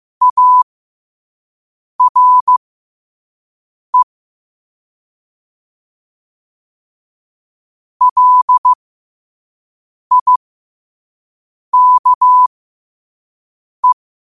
Morse code, heard as a keyed tone, about 15 wpm, spaced as for 5 wpm, 1000 Hz. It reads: ARE LIKE
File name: qotdfiles / 5wpm